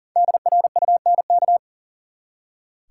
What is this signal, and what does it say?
Morse code, heard as a keyed tone, about 38 wpm, 700 Hz. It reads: DRUNK
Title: Five Letter Words - 40wpm